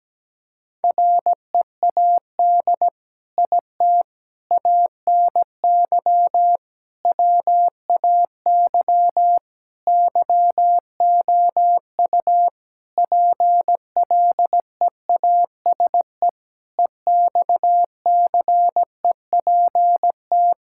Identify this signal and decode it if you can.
Morse code, heard as a keyed tone, about 17 wpm, 700 Hz. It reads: READ IT ANY WAY YOU PLEASE EXCEPT
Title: Sentences from Top 1600 Words 17wpm